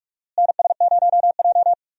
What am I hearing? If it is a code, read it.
Morse code, heard as a keyed tone, about 46 wpm, 700 Hz. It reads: NS0J
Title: USA Call Signs - 4 Characters - Repeated 2x - 45wpm